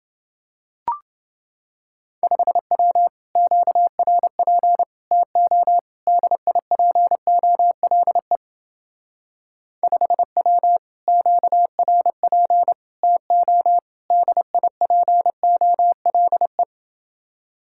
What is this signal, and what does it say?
Morse code, heard as a keyed tone, about 29 wpm, 700 Hz. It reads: 5W QRP TO DIPOLE 5W QRP TO DIPOLE